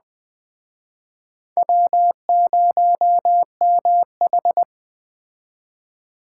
Morse code, heard as a keyed tone, about 20 wpm, 700 Hz. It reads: W0MH